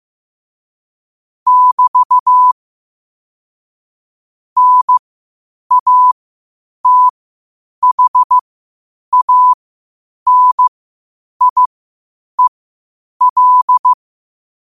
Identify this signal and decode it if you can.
Morse code, heard as a keyed tone, about 15 wpm, spaced as for 8 wpm, 1000 Hz. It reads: <BT> NATHANIEL
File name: qotdfiles / 8wpm